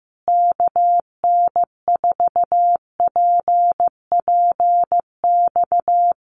Morse code, heard as a keyed tone, about 14 wpm, 700 Hz. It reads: KN4PPX
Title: Common QSO Phrases Encore 15wpm